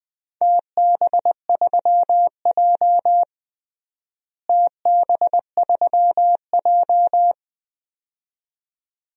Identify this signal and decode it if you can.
Morse code, heard as a keyed tone, about 21 wpm, 700 Hz. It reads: TB3J TB3J
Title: DXCC-100 Call Signs - 4 Characters - Repeated 2x - 20wpm